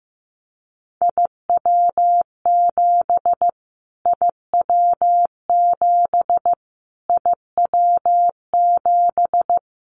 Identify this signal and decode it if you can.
Morse code, heard as a keyed tone, about 15 wpm, 700 Hz. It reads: IW7 IW7 IW7
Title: Callsign Prefixes for Top 25 DXCC - Repeated 3x - 15wpm